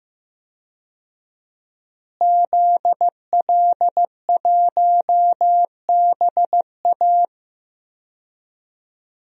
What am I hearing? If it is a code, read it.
Morse code, heard as a keyed tone, about 16 wpm, 700 Hz. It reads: ZL1BA
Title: DXCC-100 Call Signs - 5 Characters - Repeated 2x - 15wpm